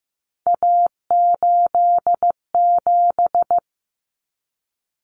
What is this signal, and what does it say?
Morse code, heard as a keyed tone, about 15 wpm, 700 Hz. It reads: A87
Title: Common Cut Numbers - 3-Digit 15wpm